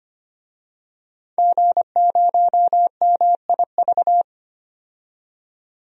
Morse code, heard as a keyed tone, about 25 wpm, 700 Hz. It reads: G0MIV
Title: DXCC-100 Call Signs - 5 Characters - 25wpm